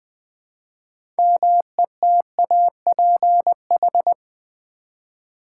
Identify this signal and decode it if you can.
Morse code, heard as a keyed tone, about 20 wpm, 700 Hz. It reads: METAPH
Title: Build A Word - 2 Letters At A Time - 8-Letter Words - 20wpm